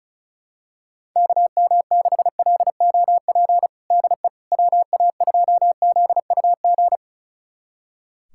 Morse code, heard as a keyed tone, about 36 wpm, 700 Hz. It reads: KM6LOP DE WA2ZUG
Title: USA Callsign DE Callsign - 6 Characters - 35wpm